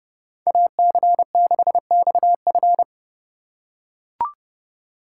Morse code, heard as a keyed tone, about 30 wpm, 700 Hz. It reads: AC6XF
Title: USA Call Signs - 5 Characters - 4x Character Spacing - 30wpm